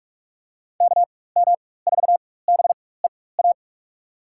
Morse code, heard as a keyed tone, about 44 wpm, spaced as for 21 wpm, 700 Hz. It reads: KM4BEA
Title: USA Call Signs - 6 Characters - 4x Character Spacing - 45wpm